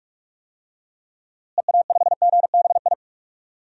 Morse code, heard as a keyed tone, about 46 wpm, 700 Hz. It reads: EA5GBI
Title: DXCC-100 Call Signs - 6 Characters - Repeated 3x - 45wpm